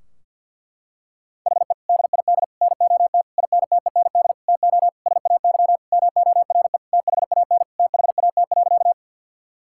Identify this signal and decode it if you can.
Morse code, heard as a keyed tone, about 50 wpm, 700 Hz. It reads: HE DID NOT INTEND TO SAY MORE THAN THAT.
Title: Sentences from Top 1600 Words 50wpm